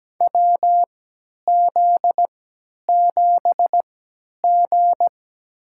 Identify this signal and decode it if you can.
Morse code, heard as a keyed tone, about 17 wpm, spaced as for 10 wpm, 700 Hz. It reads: WZ7G
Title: USA Call Signs - 3x Character Spacing - 17wpm